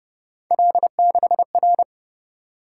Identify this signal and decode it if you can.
Morse code, heard as a keyed tone, about 29 wpm, 700 Hz. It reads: L6R
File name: Speed-Racing - Three Letter-Number Combinations - 25wpm